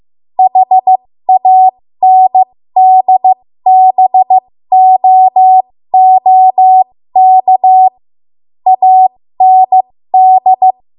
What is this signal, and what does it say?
Morse code, heard as a keyed tone, about 15 wpm, spaced as for 13 wpm, 750 Hz. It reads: HANDBOOK AND
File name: Morse / 13WPM